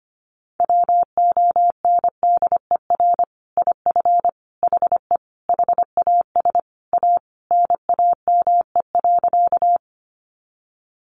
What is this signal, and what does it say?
Morse code, heard as a keyed tone, about 25 wpm, 700 Hz. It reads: WONDER IF HE HAS A NAME.
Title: Sentences from Top 500 Words 25wpm